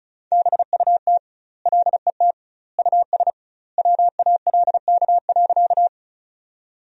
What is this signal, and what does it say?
Morse code, heard as a keyed tone, about 34 wpm, 700 Hz. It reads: BUT LET US WALK.
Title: 2x Word-Spacing - Sentences from Top 300 Words - 35wpm